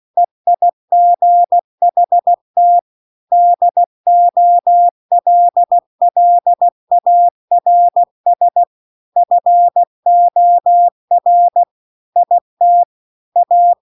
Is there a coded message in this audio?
Yes: Morse code from 0.2 to 13.7 s, heard as a keyed tone, about 16 wpm, 700 Hz. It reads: EIGHT DOLLARS FOR IT A